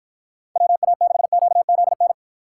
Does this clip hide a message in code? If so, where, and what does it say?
Yes: Morse code from 0.6 to 2.1 s, heard as a keyed tone, about 52 wpm, 700 Hz. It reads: WA6QZN